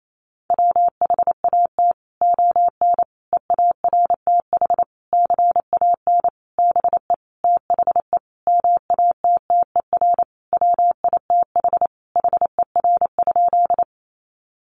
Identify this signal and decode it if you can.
Morse code, heard as a keyed tone, about 28 wpm, 700 Hz. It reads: WHAT ON EARTH CAN BE THE MATTER WITH HER?
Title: Sentences from Top 700 Words 28wpm